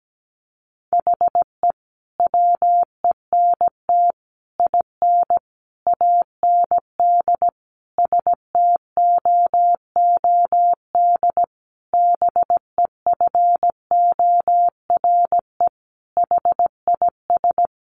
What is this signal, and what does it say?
Morse code, heard as a keyed tone, about 17 wpm, 700 Hz. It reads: HE WENT IN AND STOOD BEFORE HIS
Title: Sentences from Top 1300 Words 17wpm